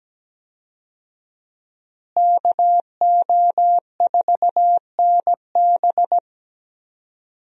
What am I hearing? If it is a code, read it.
Morse code, heard as a keyed tone, about 17 wpm, 700 Hz. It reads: KO4NB
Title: USA Call Signs - 5 Characters - Repeated 3x - 17wpm